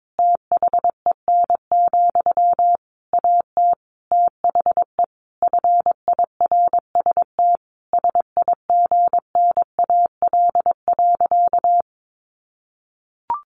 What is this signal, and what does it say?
Morse code, heard as a keyed tone, about 22 wpm, 700 Hz. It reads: THEN, AT THE FIRST SIGNAL.
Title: Sentences from Top 1400 Words 22wpm